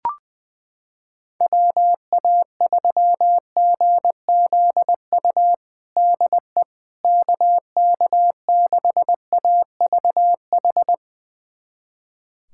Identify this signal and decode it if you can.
Morse code, heard as a keyed tone, about 20 wpm, 700 Hz. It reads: WA3GZU DE KK6AVH
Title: USA Callsign DE Callsign - 6 Characters - 20wpm